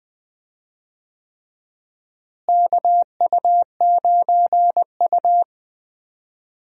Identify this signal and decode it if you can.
Morse code, heard as a keyed tone, about 21 wpm, 700 Hz. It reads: KU9U
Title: USA Call Signs - 4 Characters - Repeated 3x - 20wpm